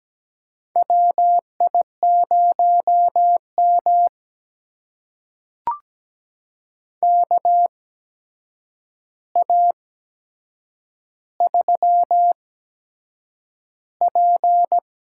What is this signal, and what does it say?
Morse code, heard as a keyed tone, about 17 wpm, spaced as for 7 wpm, 700 Hz. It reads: WI0M KA3P
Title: USA Call Signs - 8x Character Spacing - 17wpm